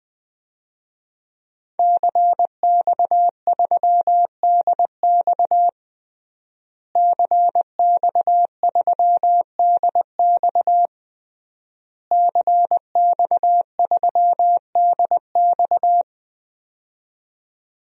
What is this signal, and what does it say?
Morse code, heard as a keyed tone, about 20 wpm, 700 Hz. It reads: CX3DX CX3DX CX3DX
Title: DXCC-100 Call Signs - 5 Characters - Repeated 3x - 20wpm